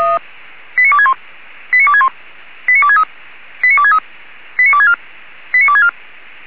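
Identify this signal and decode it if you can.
Five-tone selective call (ZVEI), heard as quick runs of five tones, tones of about 70 ms. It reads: 89261 89261 89263 89263 89264 89264
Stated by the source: ZVEI five-tone call 89283 received by SDR.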